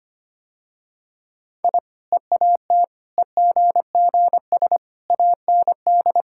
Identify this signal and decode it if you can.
Morse code, heard as a keyed tone, about 25 wpm, 700 Hz. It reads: I EAT EGGS AND
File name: Easy Sentences with Hints - 25wpm